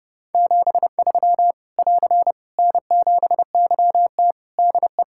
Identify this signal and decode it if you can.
Morse code, heard as a keyed tone, about 30 wpm, 700 Hz. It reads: 73 <AR> N7YT DE
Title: Practice QSO - Exchange 4 - Ending - 30wpm